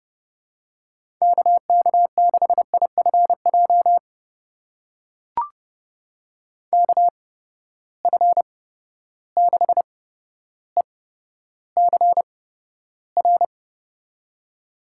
Morse code, heard as a keyed tone, about 30 wpm, spaced as for 13 wpm, 700 Hz. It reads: KK6IFJ KF6ECR
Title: USA Call Signs - 6 Characters - 8x Character Spacing - 30wpm